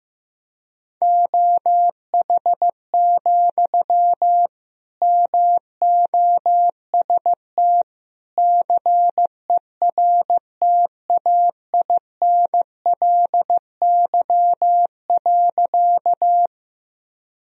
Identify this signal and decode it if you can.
Morse code, heard as a keyed tone, about 15 wpm, 700 Hz. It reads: OH, MOST CERTAINLY.
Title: Sentences from Top 1000 Words - Review Part 1 - 15wpm